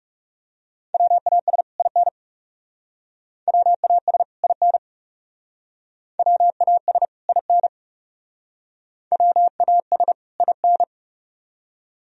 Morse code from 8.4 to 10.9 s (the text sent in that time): WAS IN